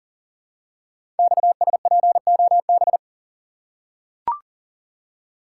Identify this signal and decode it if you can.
Morse code, heard as a keyed tone, about 42 wpm, 700 Hz. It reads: XSPOB
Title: Code Groups - 5 Letters - 40wpm